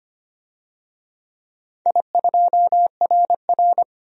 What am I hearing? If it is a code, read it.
Morse code, heard as a keyed tone, about 25 wpm, 700 Hz. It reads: I2RR
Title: DXCC-100 Call Signs - 4 Characters - 25wpm